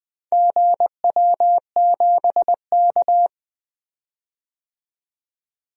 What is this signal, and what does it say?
Morse code, heard as a keyed tone, about 20 wpm, 700 Hz. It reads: GW7K
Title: DXCC-100 Call Signs - 4 Characters - 20wpm